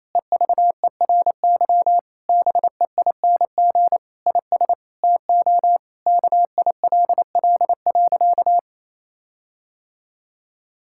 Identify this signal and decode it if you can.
Morse code, heard as a keyed tone, about 28 wpm, 700 Hz. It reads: EVERY BEING IS TO KILL.
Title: Sentences from Top 1600 Words 28wpm